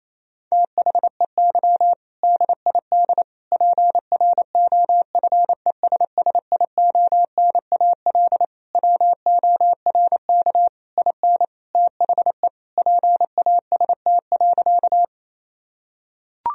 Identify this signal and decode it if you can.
Morse code, heard as a keyed tone, about 28 wpm, 700 Hz. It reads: THEY DID PROFESSIONAL WORK IN THE PAST.
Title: Sentences from Top 1000 Words - Review Part 4 - 28wpm